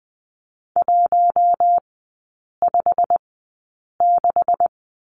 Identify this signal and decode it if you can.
Morse code, heard as a keyed tone, about 20 wpm, spaced as for 8 wpm, 700 Hz. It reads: 156
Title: 138 Lesson 23e - Letter f - Three Characters 20wpm